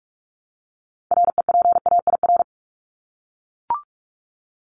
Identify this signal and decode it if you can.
Morse code, heard as a keyed tone, about 44 wpm, 700 Hz. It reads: REPAIR
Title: Top 1501-1750 Words 45wpm